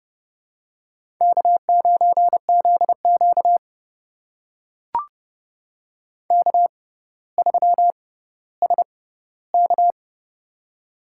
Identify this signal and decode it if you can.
Morse code, heard as a keyed tone, about 30 wpm, spaced as for 15 wpm, 700 Hz. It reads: K9ZQ K3SK
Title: USA Call Signs - 4 Characters - 6x Character Spacing - 30wpm